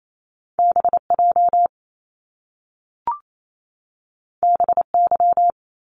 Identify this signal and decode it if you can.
Morse code, heard as a keyed tone, about 28 wpm, 700 Hz. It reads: B J B Y
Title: Top Two Letter Combinations in Words 28wpm